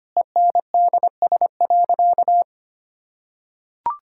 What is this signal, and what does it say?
Morse code, heard as a keyed tone, about 25 wpm, 700 Hz. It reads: ENDS.
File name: practice / 25wpm / Difficult Sentences with Hints - 25wpm